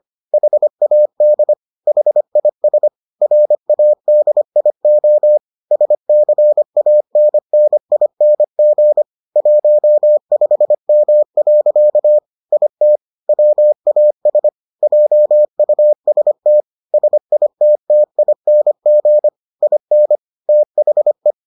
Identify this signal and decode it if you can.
Morse code, heard as a keyed tone, about 25 wpm, 600 Hz. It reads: HAD HIS RADIO SCANNING 15M. IT WAS JUST SITTING IN THE